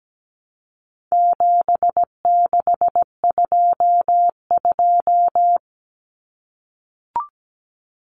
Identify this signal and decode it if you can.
Morse code, heard as a keyed tone, about 17 wpm, 700 Hz. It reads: 7622
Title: Common Cut Numbers - 4-Digit 17wpm